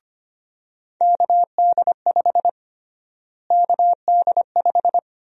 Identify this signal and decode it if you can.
Morse code, heard as a keyed tone, about 25 wpm, 700 Hz. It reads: KD5 KD5
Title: USA Call Sign Prefixes - Repeated 2x - 25wpm